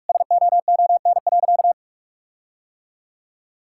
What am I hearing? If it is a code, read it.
Morse code, heard as a keyed tone, about 46 wpm, 700 Hz. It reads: SOON.
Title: Sentences from Top 1600 Words 45wpm